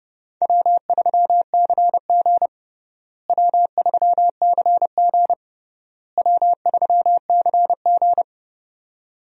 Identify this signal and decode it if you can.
Morse code, heard as a keyed tone, about 32 wpm, 700 Hz. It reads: W3CG W3CG W3CG
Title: USA Call Signs - 4 Characters - Repeated 3x - 30wpm